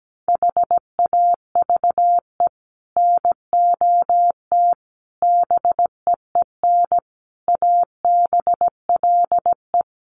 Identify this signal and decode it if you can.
Morse code, heard as a keyed tone, about 17 wpm, 700 Hz. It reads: HAVE NOT BEEN ABLE
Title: Sets of 5 Words 17wpm